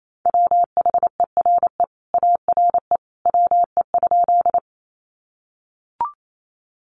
Morse code, heard as a keyed tone, about 28 wpm, 700 Hz. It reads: WHERE ARE WE?
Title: Sentences from Top 200 Words 28wpm